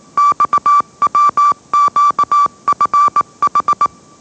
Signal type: Morse code